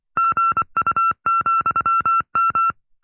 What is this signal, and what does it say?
Morse code, heard as a keyed tone, about 24 wpm, 1350 Hz. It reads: GU,M
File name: FreeFiles / CW-practice / 20WPM-side1